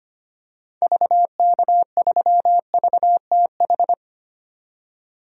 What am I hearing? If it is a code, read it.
Morse code, heard as a keyed tone, about 25 wpm, 700 Hz. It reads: VK3VTH
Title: DXCC-100 Call Signs - 6 Characters - 25wpm